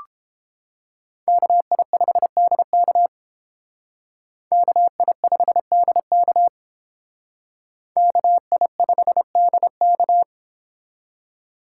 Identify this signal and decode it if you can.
Morse code, heard as a keyed tone, about 31 wpm, 700 Hz. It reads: KI5DK KI5DK KI5DK